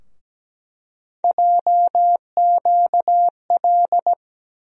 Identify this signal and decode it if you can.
Morse code, heard as a keyed tone, about 17 wpm, 700 Hz. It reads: JQL
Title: USA Call Sign Postfixes - 3 Letters 17wpm